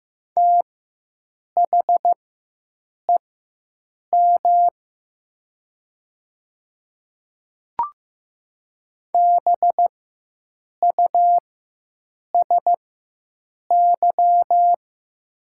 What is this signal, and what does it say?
Morse code, heard as a keyed tone, about 15 wpm, spaced as for 7 wpm, 700 Hz. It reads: THEM BUSY
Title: 4x Character Spacing - Top 500 Words - 15wpm